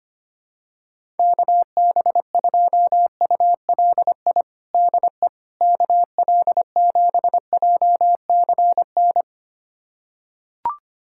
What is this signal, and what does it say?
Morse code, heard as a keyed tone, about 25 wpm, 700 Hz. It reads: KB2ULI DE KL7JCN
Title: USA Callsign DE Callsign 25wpm